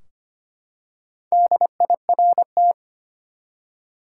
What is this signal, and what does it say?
Morse code, heard as a keyed tone, about 25 wpm, 700 Hz. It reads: DIRT
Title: Four Letter Words - 25wpm